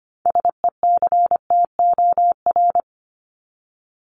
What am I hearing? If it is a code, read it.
Morse code, heard as a keyed tone, about 25 wpm, 700 Hz. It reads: SECTOR
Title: Top 1751-2000 Words 25wpm